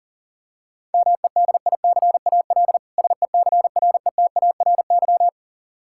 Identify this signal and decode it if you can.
Morse code, heard as a keyed tone, about 40 wpm, 700 Hz. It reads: MEDICAL SECRETARY
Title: Practice QSO - Exchange 3 - More Info - 40wpm